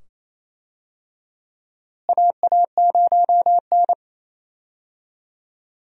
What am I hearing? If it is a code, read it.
Morse code, heard as a keyed tone, about 28 wpm, 700 Hz. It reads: AA0N
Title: USA Call Signs - 4 Characters - Repeated 2x - 28wpm